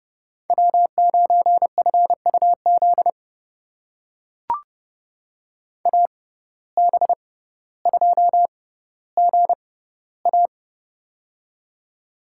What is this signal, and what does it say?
Morse code, heard as a keyed tone, about 31 wpm, spaced as for 15 wpm, 700 Hz. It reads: W9FUZ AB2GA
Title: USA Call Signs - 5 Characters - 6x Character Spacing - 30wpm